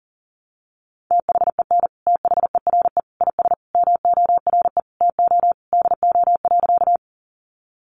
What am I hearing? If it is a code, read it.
Morse code, heard as a keyed tone, about 40 wpm, 700 Hz. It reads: THEN THERE IS MORE TO DO.
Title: Sentences from Top 100 Words 40wpm